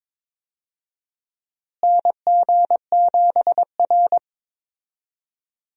Morse code, heard as a keyed tone, about 21 wpm, 700 Hz. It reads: NG7R
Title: USA Call Signs - 4 Characters - Repeated 2x - 22wpm